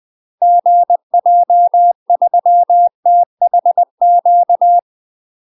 Morse code, heard as a keyed tone, about 20 wpm, 700 Hz. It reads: GJ3THQ
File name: UK MASTER QSOs TRAINING/20WPM